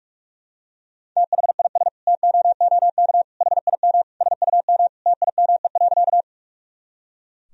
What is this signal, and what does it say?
Morse code, heard as a keyed tone, about 41 wpm, 700 Hz. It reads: THIS TOOK HIM SUM TIME.
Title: Sentences from Top 300 Words and Abbreviations 45wpm